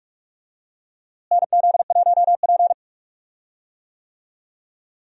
Morse code, heard as a keyed tone, about 46 wpm, 700 Hz. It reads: NZ1P
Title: USA Call Signs - 4 Characters - Repeated 3x - 45wpm